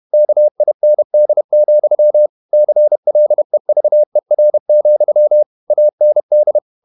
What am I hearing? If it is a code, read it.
Morse code, heard as a keyed tone, about 31 wpm, 600 Hz. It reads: KIND, CLEVER, AND
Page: Never 31wpm | CW med Gnister